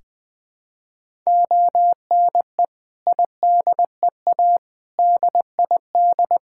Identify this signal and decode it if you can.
Morse code, heard as a keyed tone, about 20 wpm, 700 Hz. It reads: ONE IDEA DID
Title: Sentences from Top 1000 Words - Review Part 6 - 20wpm